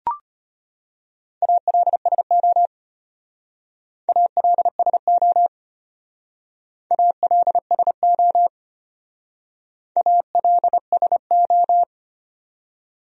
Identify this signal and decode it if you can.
Morse code, heard as a keyed tone, about 32 wpm, 700 Hz. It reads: ALSO ALSO ALSO ALSO